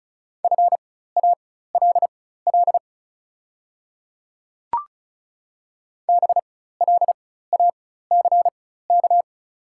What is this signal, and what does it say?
Morse code, heard as a keyed tone, about 36 wpm, spaced as for 16 wpm, 690 Hz. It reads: FALL BLACK
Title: 4x Character Spacing - Top 500 Words - 35wpm